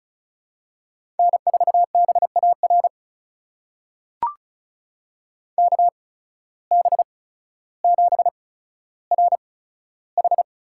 Morse code, heard as a keyed tone, about 36 wpm, spaced as for 15 wpm, 700 Hz. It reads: N4BAR KB7RH
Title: USA Call Signs - 8x Character Spacing - 35wpm